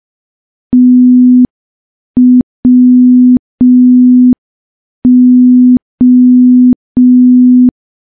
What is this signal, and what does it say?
Morse code, heard as a keyed tone, about 5 wpm, 250 Hz. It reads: TWO
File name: morse / morse-training-05WPM